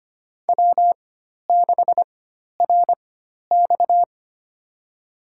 Morse code, heard as a keyed tone, about 25 wpm, spaced as for 12 wpm, 700 Hz. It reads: W6RX
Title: USA Call Signs - 4 Characters - 4x Character Spacing - 25wpm